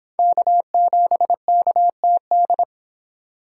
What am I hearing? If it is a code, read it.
Morse code, heard as a keyed tone, about 26 wpm, 700 Hz. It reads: K7KTD